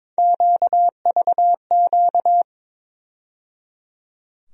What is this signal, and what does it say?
Morse code, heard as a keyed tone, about 22 wpm, 700 Hz. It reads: QVQ
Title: USA Call Sign Postfixes - 3 Letters 22wpm